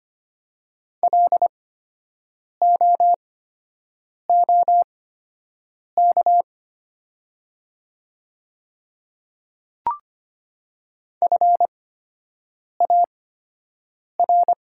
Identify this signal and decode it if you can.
Morse code, heard as a keyed tone, about 25 wpm, spaced as for 7 wpm, 700 Hz. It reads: LOOK FAR
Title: 8x Character Spacing - Top 300 Words - 25wpm